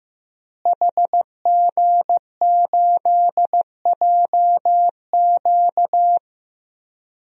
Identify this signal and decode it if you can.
Morse code, heard as a keyed tone, about 16 wpm, 700 Hz. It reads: HG8JQ
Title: DXCC-100 Call Signs - 5 Characters - Repeated 3x - 15wpm